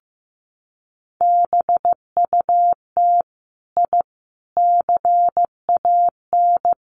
Morse code, heard as a keyed tone, about 15 wpm, 700 Hz. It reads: BUT I CAN
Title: Sets of 5 Words Encore 15wpm